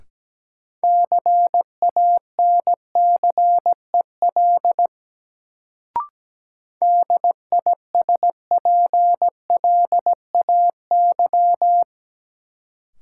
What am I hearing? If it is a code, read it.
Morse code, heard as a keyed tone, about 17 wpm, 700 Hz. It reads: CANCEL DISPLAY
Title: Sentences from Top 1200 Words - New Word Drill - 17wpm